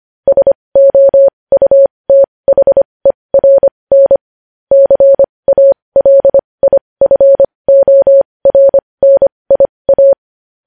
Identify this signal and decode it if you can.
Morse code, heard as a keyed tone, about 25 wpm, spaced as for 20 wpm, 550 Hz. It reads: SOUTHERN CALIFORNIA
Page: news headlines in morse code at 20 wpm